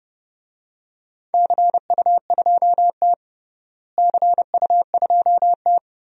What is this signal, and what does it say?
Morse code, heard as a keyed tone, about 30 wpm, 700 Hz. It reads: CU2T CU2T
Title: DXCC-100 Call Signs - 4 Characters - Repeated 3x - 30wpm